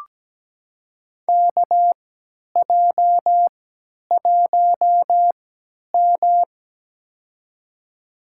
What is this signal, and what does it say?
Morse code, heard as a keyed tone, about 17 wpm, spaced as for 10 wpm, 700 Hz. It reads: KJ1M
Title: USA Call Signs - 3x Character Spacing - 17wpm